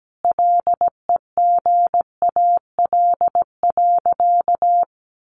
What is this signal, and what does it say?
Morse code, heard as a keyed tone, about 17 wpm, 700 Hz. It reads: LEGAL.
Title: Sentences from Top 1200 Words 17wpm